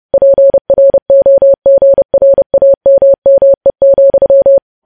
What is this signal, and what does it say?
Morse code, heard as a keyed tone, about 30 wpm, 550 Hz. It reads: PROGRAMME,
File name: podcast / News headlines-1x-30x30